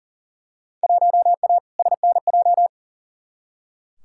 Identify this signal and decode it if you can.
Morse code, heard as a keyed tone, about 40 wpm, 700 Hz. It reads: 1A SNJ
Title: Field Day Exchanges - 40wpm